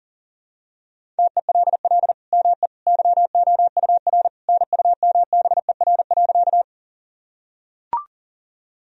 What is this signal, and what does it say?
Morse code, heard as a keyed tone, about 40 wpm, 700 Hz. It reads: TELL ME YOUR NUMBER.
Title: Sentences from Top 1000 Words - Review Part 1 - 40wpm